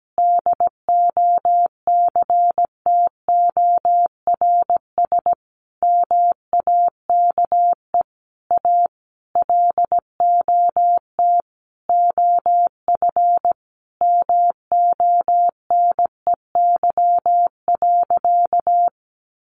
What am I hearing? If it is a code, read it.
Morse code, heard as a keyed tone, about 17 wpm, 700 Hz. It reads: DOCTORS MAKE A LOT OF MONEY.
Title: Easy Phrases 17wpm